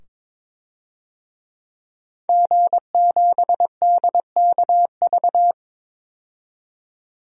Morse code, heard as a keyed tone, about 22 wpm, 700 Hz. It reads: G7DKV
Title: DXCC-100 Call Signs - 5 Characters - 22wpm